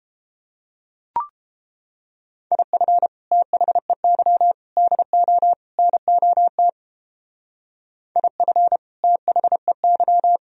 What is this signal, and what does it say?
Morse code, heard as a keyed tone, about 29 wpm, 700 Hz. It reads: IF THEY DO NOT IF THEY